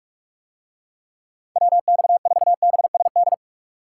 Morse code, heard as a keyed tone, about 47 wpm, 700 Hz. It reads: WX4BSD